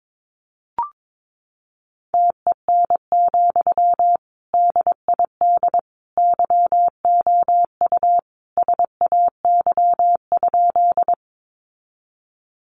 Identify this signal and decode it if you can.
Morse code, heard as a keyed tone, about 22 wpm, 700 Hz. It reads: TEN, DID YOU SAY?
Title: Sentences from Top 400 Words 22wpm